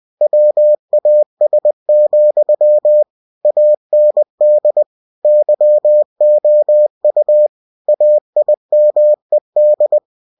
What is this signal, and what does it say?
Morse code, heard as a keyed tone, about 20 wpm, 600 Hz. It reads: WAS, AND YOU AIMED